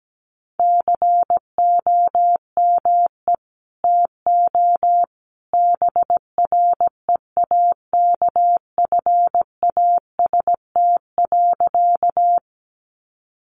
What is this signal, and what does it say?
Morse code, heard as a keyed tone, about 17 wpm, 700 Hz. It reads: COME TO BREAKFAST.
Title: Sentences from Top 1500 Words 17wpm